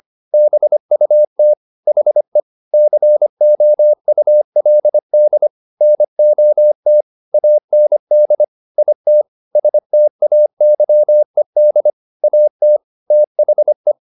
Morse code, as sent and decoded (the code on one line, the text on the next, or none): -... ..- - / .... / -.-. --- ..- .-.. -.. / -. --- - / .- -. -.. / .. - / ... - .- -.-- . -.. / .- - / - .... .
BUT HE COULD NOT AND IT STAYED AT THE